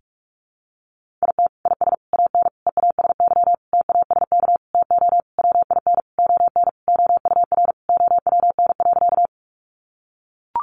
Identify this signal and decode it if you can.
Morse code, heard as a keyed tone, about 44 wpm, 700 Hz. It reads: IT IS AN EASY TASK TO WIN ON OUR OWN.
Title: Sentences from Top 1500 Words 45wpm